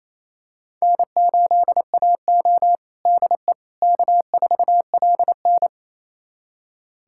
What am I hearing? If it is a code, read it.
Morse code, heard as a keyed tone, about 28 wpm, 700 Hz. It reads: N8AO DE K4LN
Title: USA Callsign DE Callsign - 4 Characters - 28wpm